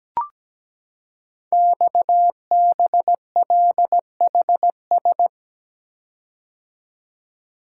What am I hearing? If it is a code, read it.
Morse code, heard as a keyed tone, about 17 wpm, 700 Hz. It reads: XBLHS
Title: Code Groups - 5 Letters - 17wpm